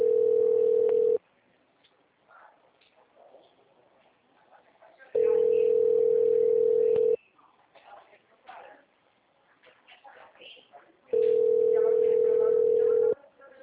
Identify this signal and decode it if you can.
Ringback tone (North America), 440 + 480 Hz, with speech.
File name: out- -2008